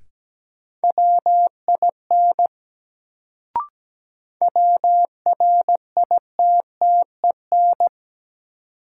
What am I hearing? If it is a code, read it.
Morse code, heard as a keyed tone, about 17 wpm, 700 Hz. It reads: WIN WRITTEN
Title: Sentences from Top 1500 Words - New Word Drill Alt - 17wpm